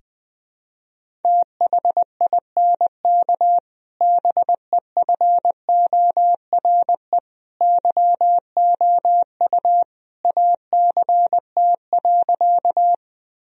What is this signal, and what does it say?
Morse code, heard as a keyed tone, about 20 wpm, 700 Hz. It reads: THINK BEFORE YOU ACT.
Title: Sentences from Top 1000 Words - Review Part 1 - 20wpm